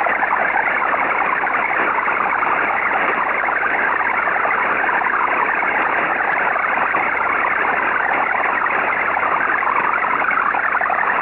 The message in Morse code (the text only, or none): none